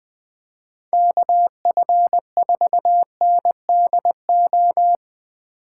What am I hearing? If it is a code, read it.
Morse code, heard as a keyed tone, about 20 wpm, 700 Hz. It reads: KF4NDO
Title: Speed Racing - USA 2x3 Call Signs 17wpm